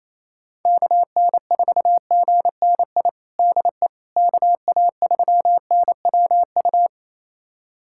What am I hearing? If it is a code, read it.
Morse code, heard as a keyed tone, about 28 wpm, 700 Hz. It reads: KN4GNI DE KA3NWU
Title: USA Callsign DE Callsign - 6 Characters - 28wpm